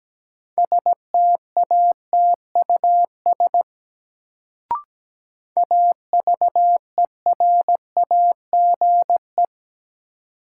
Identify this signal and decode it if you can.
Morse code, heard as a keyed tone, about 17 wpm, 700 Hz. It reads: STATUS AVERAGE
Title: Sentences from Top 1400 Words - New Word Drill Alt - 17wpm